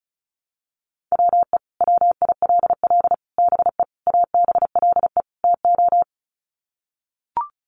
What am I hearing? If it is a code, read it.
Morse code, heard as a keyed tone, about 34 wpm, 700 Hz. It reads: WE WILL BE ABLE TO
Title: Sets of 5 Words Encore 35wpm